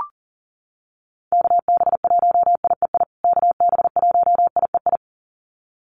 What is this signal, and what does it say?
Morse code, heard as a keyed tone, about 40 wpm, 700 Hz. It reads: KB1IEI KB1IEI
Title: USA Call Signs Repeated 2x 40wpm